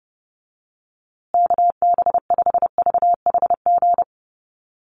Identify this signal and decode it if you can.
Morse code, heard as a keyed tone, about 30 wpm, 700 Hz. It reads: KB5VHG
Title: USA Call Signs Repeated 3x 30wpm